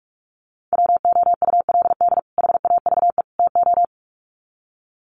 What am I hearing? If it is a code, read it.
Morse code, heard as a keyed tone, about 44 wpm, 700 Hz. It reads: WOULD HAVE TO
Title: Sets of 3 Words 45wpm